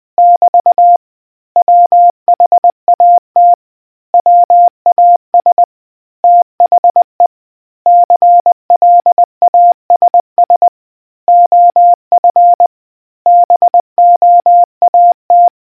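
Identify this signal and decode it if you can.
Morse code, heard as a keyed tone, about 20 wpm, 700 Hz. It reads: <BT> WHAT WAS THE CLASS OF BOAT
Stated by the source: Morse quiz question for naval history buffs from the RNZN Communicators Association.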